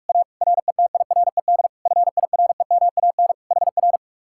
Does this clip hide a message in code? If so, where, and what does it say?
Yes: Morse code from 0.1 to 4.0 s, heard as a keyed tone, about 45 wpm, 700 Hz. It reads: A RETIRED FIREMAN HR